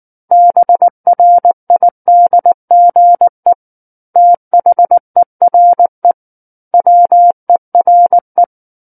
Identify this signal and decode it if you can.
Morse code, heard as a keyed tone, about 19 wpm, 700 Hz. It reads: BRIDGE THERE WERE
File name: Book One / 10-Chapter 10 - 19 WPM